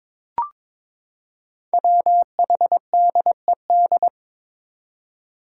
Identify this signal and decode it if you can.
Morse code, heard as a keyed tone, about 22 wpm, 700 Hz. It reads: WHDED
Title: Five Challenging Letter-Number Combinations 22wpm